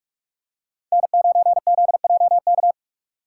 Morse code, heard as a keyed tone, about 46 wpm, 700 Hz. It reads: N9ZJK
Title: USA Call Signs - 5 Characters - 6x Character Spacing - 45wpm